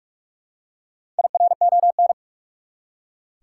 Morse code, heard as a keyed tone, about 46 wpm, 700 Hz. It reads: IRON
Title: Four Letter Words - 45wpm